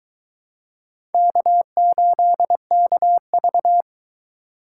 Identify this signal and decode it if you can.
Morse code, heard as a keyed tone, about 23 wpm, 700 Hz. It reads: K8KV